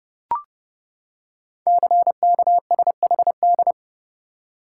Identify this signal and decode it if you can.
Morse code, heard as a keyed tone, about 30 wpm, 700 Hz. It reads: CKSHD
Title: Code Groups - 5 Letters - 30wpm